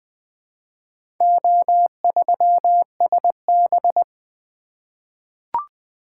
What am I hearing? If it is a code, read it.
Morse code, heard as a keyed tone, about 20 wpm, 700 Hz. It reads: O3SB
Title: Four Letter-Number Combinations - 20wpm